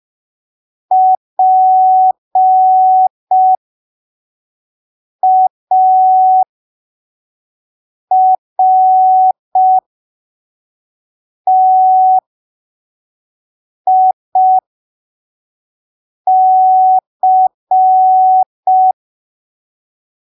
Morse code, heard as a keyed tone, about 5 wpm, spaced as for 3 wpm, 750 Hz. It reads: PARTIC
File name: arquivos / texto / texto 25 05ppm